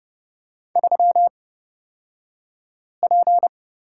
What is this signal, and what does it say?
Morse code, heard as a keyed tone, about 30 wpm, spaced as for 5 wpm, 700 Hz. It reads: 3P
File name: Lesson 37d - Letter x - Two Characters 30wpm